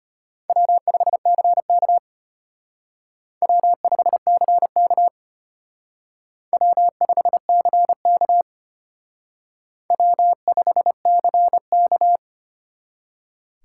Morse code, heard as a keyed tone, about 32 wpm, 700 Hz. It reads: W5CK W5CK W5CK W5CK